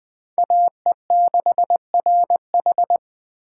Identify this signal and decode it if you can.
Morse code, heard as a keyed tone, about 20 wpm, 700 Hz. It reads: AE6RH
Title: Speed Racing - USA 2x2 Call Signs 17wpm